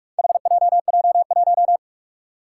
Morse code, heard as a keyed tone, about 46 wpm, 700 Hz. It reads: HJJ1
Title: Four Challenging Letter-Number Combinations 45wpm